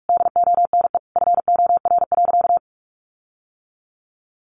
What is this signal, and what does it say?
Morse code, heard as a keyed tone, about 44 wpm, 700 Hz. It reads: DONE FOR.
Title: Sentences from Top 1500 Words 45wpm